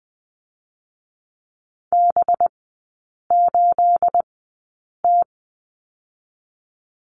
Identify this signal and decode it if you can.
Morse code, heard as a keyed tone, about 20 wpm, spaced as for 8 wpm, 700 Hz. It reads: B8T
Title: 218 Lesson 35e - Number 0 - Three Characters 20wpm